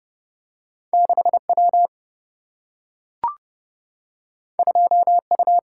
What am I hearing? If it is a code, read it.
Morse code, heard as a keyed tone, about 30 wpm, 700 Hz. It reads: 6 W 2 U
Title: Single-Letter Single-Number 30wpm